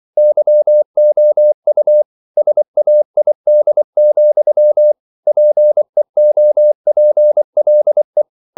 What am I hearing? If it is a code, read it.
Morse code, heard as a keyed tone, about 24 wpm, 600 Hz. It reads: YOU SAID, PEOPLE
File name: CW / Practice / EnglishQuotations0224